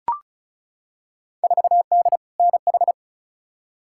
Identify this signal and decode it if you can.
Morse code, heard as a keyed tone, about 36 wpm, 700 Hz. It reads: 4D NH
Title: Field Day Exchanges Encore - 35wpm